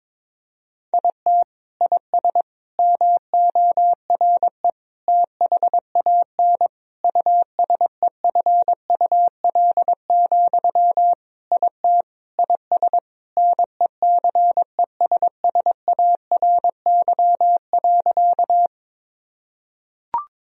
Morse code, heard as a keyed tone, about 22 wpm, 700 Hz. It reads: IT IS MORE THAN USEFUL, IT IS NECESSARY.
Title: Sentences from Top 1000 Words - Review Part 4 - 22wpm